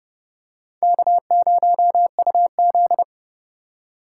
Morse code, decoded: K0UZ

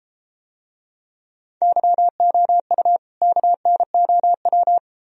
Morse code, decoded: YOU KNOW

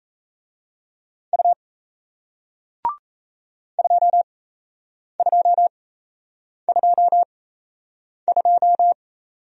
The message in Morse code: U 2222

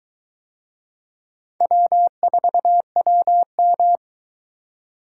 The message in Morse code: W4WM

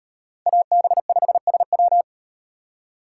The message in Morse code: AB5SW

